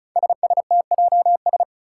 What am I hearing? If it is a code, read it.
Morse code, heard as a keyed tone, about 36 wpm, 700 Hz. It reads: SSTJS
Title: Five Challenging Letter-Number Combinations 35wpm